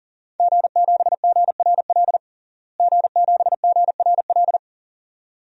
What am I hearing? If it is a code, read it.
Morse code, heard as a keyed tone, about 37 wpm, 700 Hz. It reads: G7GRL G7GRL
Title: DXCC-100 Call Signs - 5 Characters - Repeated 2x - 40wpm